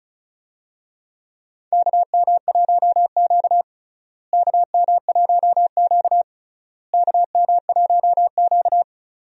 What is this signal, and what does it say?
Morse code, heard as a keyed tone, about 36 wpm, 700 Hz. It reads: KM1Q KM1Q KM1Q
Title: USA Call Signs - 4 Characters - Repeated 3x - 35wpm